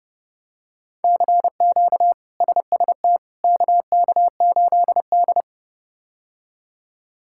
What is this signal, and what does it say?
Morse code, heard as a keyed tone, about 30 wpm, 700 Hz. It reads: CQ SST KK8D
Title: K1USN SST QSO Practice - 30wpm